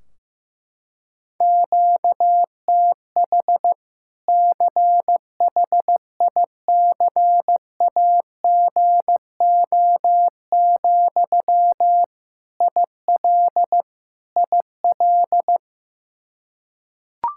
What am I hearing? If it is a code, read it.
Morse code, heard as a keyed tone, about 15 wpm, 700 Hz. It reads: QTH CHICAGO, IL IL
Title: Practice QSO - Exchange 1 - Beginning - 15wpm